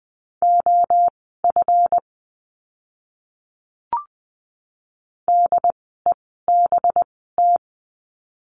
Morse code, heard as a keyed tone, about 20 wpm, spaced as for 14 wpm, 700 Hz. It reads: OF DEBT
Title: 185 Lesson 30f - Letter b - Words 20wpm